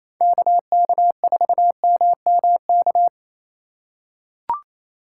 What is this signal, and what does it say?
Morse code, heard as a keyed tone, about 28 wpm, 700 Hz. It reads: KK4MMK
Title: USA Call Signs - 2x Character Spacing - 28wpm